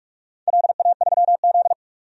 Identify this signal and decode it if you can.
Morse code, heard as a keyed tone, about 48 wpm, 700 Hz. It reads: LA3Z
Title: DXCC-100 Call Signs - 4 Characters - 45wpm